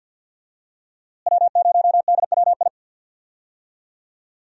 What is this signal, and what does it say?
Morse code, heard as a keyed tone, about 53 wpm, 700 Hz. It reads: W0DWI
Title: USA Call Signs - 5 Characters - Repeated 2x - 50wpm